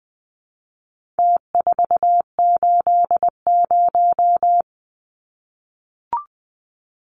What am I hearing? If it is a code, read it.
Morse code, heard as a keyed tone, about 20 wpm, 700 Hz. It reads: T480
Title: Numbers - Contest Style - 4-Digit 20wpm